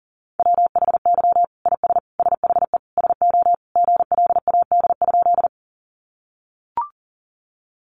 Morse code, heard as a keyed tone, about 40 wpm, 700 Hz. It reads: WHY IS SHE SO GLAD?
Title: Sentences from Top 800 Words 40wpm